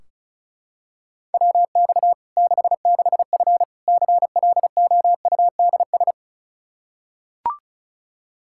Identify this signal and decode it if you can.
Morse code, heard as a keyed tone, about 34 wpm, 700 Hz. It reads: WX 66F CLOUDS
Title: Practice QSO - Exchange 2 - Info - 35wpm